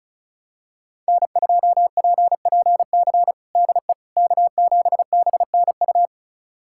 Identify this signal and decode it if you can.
Morse code, heard as a keyed tone, about 34 wpm, 700 Hz. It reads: N2PPC DE K7BNU
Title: USA Callsign DE Callsign - 5 Characters - 35wpm